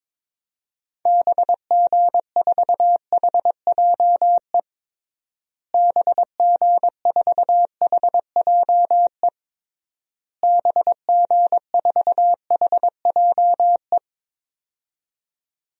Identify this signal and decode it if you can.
Morse code, heard as a keyed tone, about 22 wpm, 700 Hz. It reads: BG4HJE BG4HJE BG4HJE
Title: DXCC-100 Call Signs - 6 Characters - Repeated 3x - 22wpm